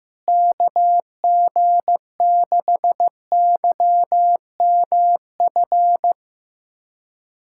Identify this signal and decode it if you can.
Morse code, heard as a keyed tone, about 15 wpm, 700 Hz. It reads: KG6YMF
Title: USA Callsign DE Callsign 15wpm